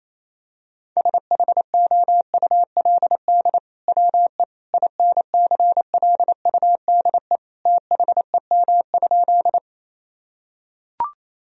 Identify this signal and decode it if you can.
Morse code, heard as a keyed tone, about 28 wpm, 700 Hz. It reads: SHOULD WE INCLUDE THEM?
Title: Sentences from Top 1000 Words - Review Part 3 - 28wpm